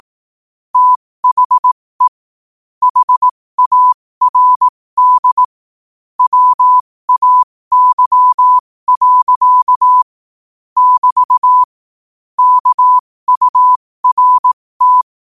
Morse code, heard as a keyed tone, about 18 wpm, spaced as for 15 wpm, 1000 Hz. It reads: THE HARD WAY. <BT> KURT
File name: qotdfiles / 15wpm